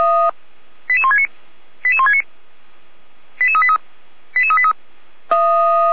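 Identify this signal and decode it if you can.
Five-tone selective call (ZVEI), heard as quick runs of five tones, tones of about 70 ms. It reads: 88169 88169 88383 88383
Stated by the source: ZVEI five-tone call received with an SDR, from the Feuerwehr Orlamünde audio server.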